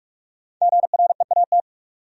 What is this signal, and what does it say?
Morse code, heard as a keyed tone, about 58 wpm, 700 Hz. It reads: GREAT